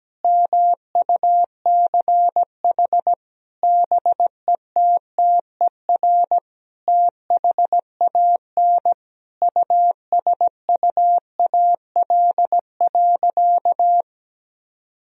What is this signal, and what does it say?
Morse code, heard as a keyed tone, about 17 wpm, 700 Hz. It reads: MUCH BETTER THAN USUAL.
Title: Sentences from Top 1000 Words - Review Part 2 - 17wpm